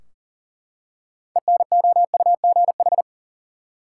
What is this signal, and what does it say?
Morse code, heard as a keyed tone, about 40 wpm, 700 Hz. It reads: ENOUGH
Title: Six Letter Words - 40wpm